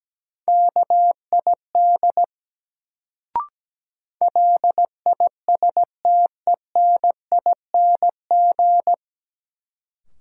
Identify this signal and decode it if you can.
Morse code, heard as a keyed tone, about 17 wpm, 700 Hz. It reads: KID LISTENING
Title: Sets of 5 Words - New Word Drill - 17wpm